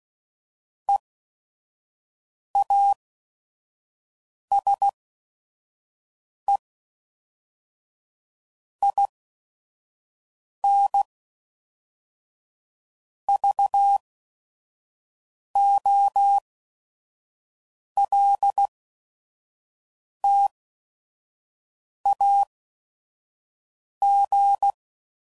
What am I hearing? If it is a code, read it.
Morse code, heard as a keyed tone, about 16 wpm, 800 Hz. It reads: E A S E I N V O L T A G